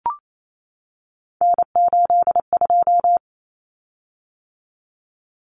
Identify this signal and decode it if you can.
Morse code, heard as a keyed tone, about 28 wpm, 700 Hz. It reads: N82
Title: Common Cut Numbers - 3-Digit 28wpm